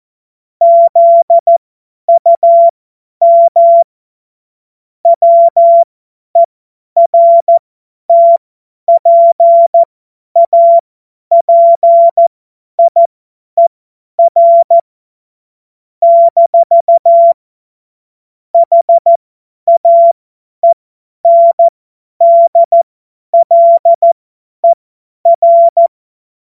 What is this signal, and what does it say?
Morse code, heard as a keyed tone, about 14 wpm, spaced as for 10 wpm, 680 Hz. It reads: ZUM WERTPAPIER - HAENDLER
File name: Witz-22-Juli-2025-50-F70